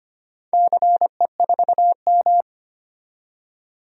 Morse code, heard as a keyed tone, about 25 wpm, 700 Hz. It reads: CE4M
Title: DXCC-100 Call Signs - 4 Characters - 25wpm